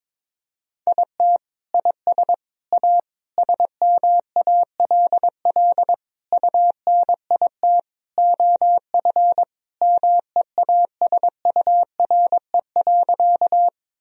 Morse code, heard as a keyed tone, about 22 wpm, 700 Hz. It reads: IT IS A SMALL UNIT OF MEASURE.
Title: Sentences from Top 1000 Words - Review Part 2 - 22wpm